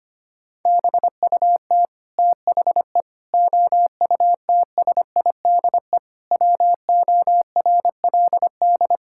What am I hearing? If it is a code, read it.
Morse code, heard as a keyed tone, about 25 wpm, 700 Hz. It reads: BUT THE OUTSIDE WORLD
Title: Sentences from Top 1000 Words - Review Part 6 - 25wpm